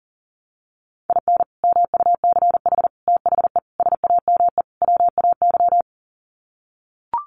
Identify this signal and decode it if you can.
Morse code, heard as a keyed tone, about 37 wpm, 700 Hz. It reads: IN MUCH THE SAME WAY
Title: Sets of 5 Words Encore 40wpm